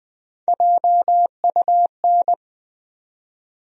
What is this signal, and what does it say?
Morse code, heard as a keyed tone, about 20 wpm, 700 Hz. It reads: JUN
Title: USA Call Sign Postfixes - 3 Letters 20wpm